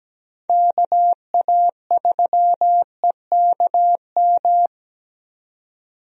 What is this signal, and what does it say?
Morse code, heard as a keyed tone, about 17 wpm, 700 Hz. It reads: KA3EKM
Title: USA Call Signs - 6 Characters - 17wpm